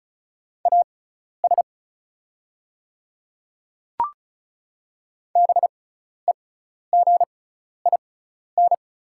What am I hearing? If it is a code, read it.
Morse code, heard as a keyed tone, about 34 wpm, spaced as for 12 wpm, 700 Hz. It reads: AS BEGIN
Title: 6x Character Spacing - Top 500 Words - 35wpm